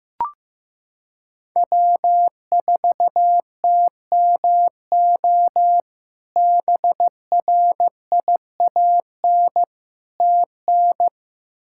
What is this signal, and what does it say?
Morse code, heard as a keyed tone, about 15 wpm, 700 Hz. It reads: W4TMO BRIAN TN
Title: K1USN SST QSO Practice - 15wpm